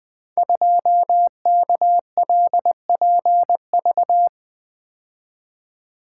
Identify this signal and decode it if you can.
Morse code, heard as a keyed tone, about 20 wpm, 700 Hz. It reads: 2KLPV
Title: Code Groups - 5 Letter:Numbers - 20wpm